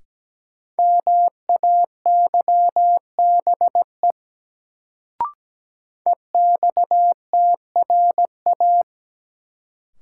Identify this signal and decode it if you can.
Morse code, heard as a keyed tone, about 17 wpm, 700 Hz. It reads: MAYBE EXTRA
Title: Sentences from Top 1000 Words - New Word Drill - 17wpm